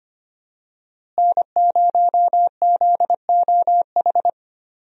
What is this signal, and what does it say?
Morse code, heard as a keyed tone, about 25 wpm, 700 Hz. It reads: N0ZOH
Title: USA Call Signs - 5 Characters - Repeated 2x - 25wpm